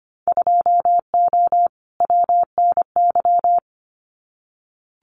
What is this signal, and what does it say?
Morse code, heard as a keyed tone, about 25 wpm, 700 Hz. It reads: 2O WNY
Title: Winter Field Day Exchanges Encore 25wpm